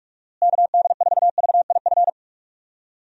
Morse code, heard as a keyed tone, about 42 wpm, 700 Hz. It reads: KD4VIF
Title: USA Call Signs - 6 Characters - Repeated 3x - 45wpm